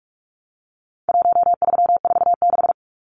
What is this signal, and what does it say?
Morse code, heard as a keyed tone, about 44 wpm, 700 Hz. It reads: 1346
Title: Numbers - Contest Style - 4-Digit 45wpm